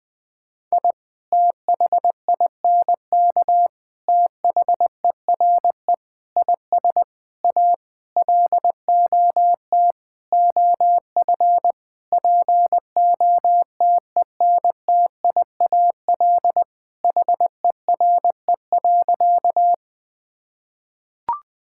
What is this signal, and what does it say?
Morse code, heard as a keyed tone, about 20 wpm, 700 Hz. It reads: I THINK THERE IS A LOT OF POTENTIAL HERE.
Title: Sentences with 6-Word Phrases - 20wpm